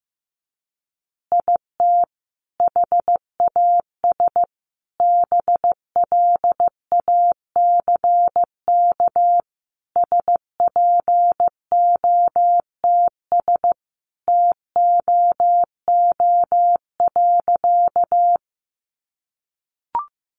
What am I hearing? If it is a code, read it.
Morse code, heard as a keyed tone, about 15 wpm, 700 Hz. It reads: IT HAS BLACK SPOTS TOO.
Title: Additional Easy Phrases 15wpm